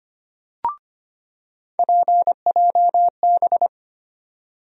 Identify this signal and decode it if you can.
Morse code, heard as a keyed tone, about 25 wpm, 700 Hz. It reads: PJB
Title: USA Call Sign Postfixes - 3 Letters 25wpm